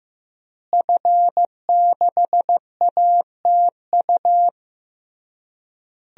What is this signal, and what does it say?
Morse code, heard as a keyed tone, about 15 wpm, 700 Hz. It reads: F6ATU
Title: DXCC-100 Call Signs - 5 Characters - Repeated 3x - 15wpm